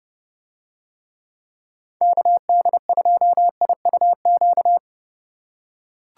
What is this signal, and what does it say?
Morse code, heard as a keyed tone, about 27 wpm, 700 Hz. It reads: KD2IUQ